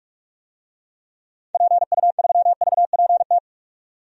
Morse code, heard as a keyed tone, about 46 wpm, 700 Hz. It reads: PU3VPT
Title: DXCC-100 Call Signs - 6 Characters - 45wpm